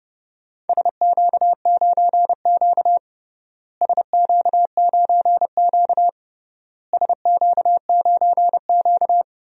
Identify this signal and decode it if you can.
Morse code, heard as a keyed tone, about 30 wpm, 700 Hz. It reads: SQ9Q SQ9Q SQ9Q
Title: DXCC-100 Call Signs - 4 Characters - Repeated 3x - 30wpm